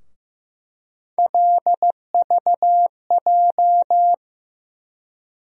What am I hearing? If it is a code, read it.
Morse code, heard as a keyed tone, about 15 wpm, 700 Hz. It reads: LVJ
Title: USA Call Sign Postfixes - 3 Letters 15wpm